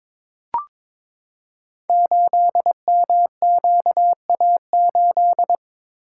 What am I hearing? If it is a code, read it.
Morse code, heard as a keyed tone, about 22 wpm, 700 Hz. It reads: 8MQA8
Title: Code Groups - 5 Letter:Numbers - 22wpm